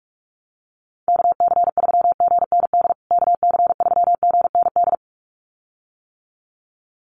Morse code, heard as a keyed tone, about 44 wpm, 700 Hz. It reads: KC3GND KC3GND
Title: USA Call Signs Repeated 2x 45wpm